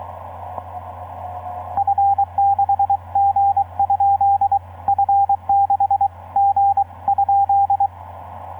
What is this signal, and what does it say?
Morse code, heard as a keyed tone, about 22 wpm, 800 Hz, with radio noise. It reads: F6G? F6G?
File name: zd9t f6gnz etc 15m cw 22nov2010 1145